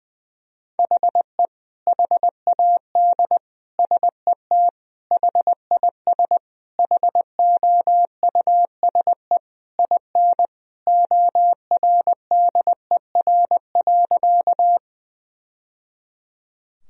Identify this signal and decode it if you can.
Morse code, heard as a keyed tone, about 20 wpm, 700 Hz. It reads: HE HAD SET HIS HOUSE IN ORDER.
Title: Sentences from Top 1000 Words - Review Part 2 - 20wpm